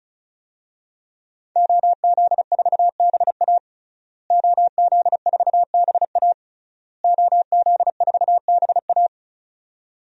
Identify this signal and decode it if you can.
Morse code, heard as a keyed tone, about 36 wpm, 700 Hz. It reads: OZ4BA OZ4BA OZ4BA
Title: DXCC-100 Call Signs - 5 Characters - Repeated 3x - 35wpm